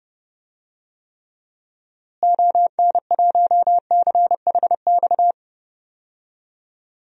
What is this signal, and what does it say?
Morse code, heard as a keyed tone, about 30 wpm, 700 Hz. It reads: ON1CHX
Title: DXCC-100 Call Signs - 6 Characters - Repeated 2x - 30wpm